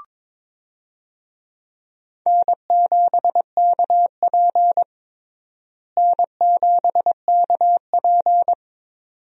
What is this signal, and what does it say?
Morse code, heard as a keyed tone, about 22 wpm, 700 Hz. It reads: N7KP N7KP
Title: USA Call Signs - 4 Characters - Repeated 2x - 22wpm